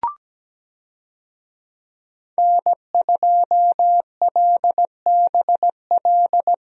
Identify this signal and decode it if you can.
Morse code, heard as a keyed tone, about 17 wpm, 700 Hz. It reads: N2LBL
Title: USA Call Signs - 5 Characters - 17wpm